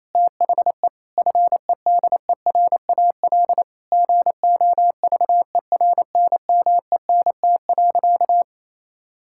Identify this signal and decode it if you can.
Morse code, heard as a keyed tone, about 28 wpm, 700 Hz. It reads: THE FEDERAL GOVERNMENT.
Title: Sentences from Top 1000 Words - Review Part 5 - 28wpm